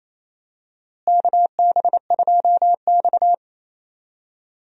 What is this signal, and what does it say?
Morse code, heard as a keyed tone, about 27 wpm, 700 Hz. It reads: KB2X
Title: USA Call Signs - 4 Characters - Repeated 3x - 28wpm